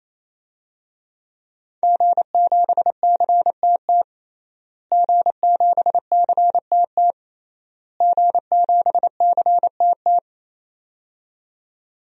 Morse code, heard as a keyed tone, about 28 wpm, 700 Hz. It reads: G7CTT G7CTT G7CTT
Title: DXCC-100 Call Signs - 5 Characters - Repeated 3x - 28wpm